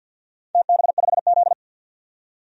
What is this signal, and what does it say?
Morse code, heard as a keyed tone, about 51 wpm, 700 Hz. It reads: TB5Z
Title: DXCC-100 Call Signs - 4 Characters - 50wpm